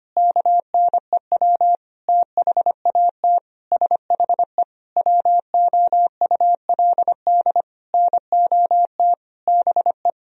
Morse code, heard as a keyed tone, about 25 wpm, 700 Hz. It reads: KNEW THAT SHE WOULD NOT BE
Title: Sentences from Top 1000 Words - Review Part 2 - 25wpm